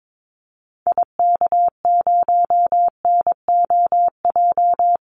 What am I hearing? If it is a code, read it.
Morse code, heard as a keyed tone, about 22 wpm, 700 Hz. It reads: IK0NOJ
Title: Call Signs Repeated 2x - 22wpm